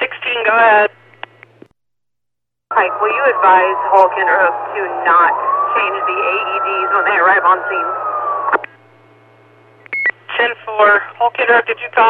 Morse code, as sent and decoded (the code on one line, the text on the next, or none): none